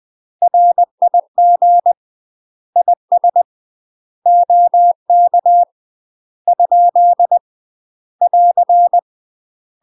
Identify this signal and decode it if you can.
Morse code, heard as a keyed tone, about 20 wpm, 700 Hz. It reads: RIG IS OK ? <AR>
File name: UK MASTER QSOs TRAINING / 20WPM / UK MASTER QSOs 0041